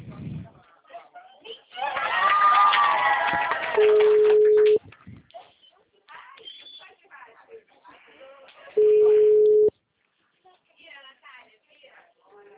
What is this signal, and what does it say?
Ringback tone (Europe, 425 Hz), with speech.